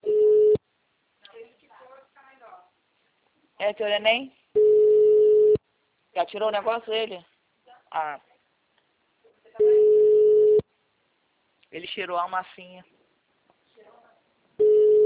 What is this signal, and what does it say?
Ringback tone (Europe, 425 Hz), with speech.
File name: out- -2002- - - .36947